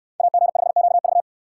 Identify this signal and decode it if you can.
Morse code, heard as a keyed tone, about 65 wpm, 700 Hz. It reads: AG5QV